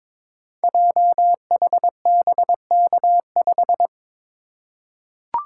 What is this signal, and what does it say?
Morse code, heard as a keyed tone, about 22 wpm, 700 Hz. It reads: JHBK5
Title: Code Groups - 5 Letter:Numbers - 22wpm